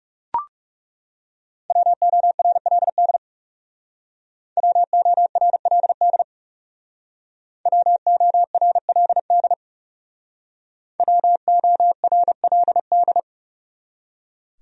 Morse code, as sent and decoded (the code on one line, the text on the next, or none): .-- --- .-. .-.. -.. / .-- --- .-. .-.. -.. / .-- --- .-. .-.. -.. / .-- --- .-. .-.. -..
WORLD WORLD WORLD WORLD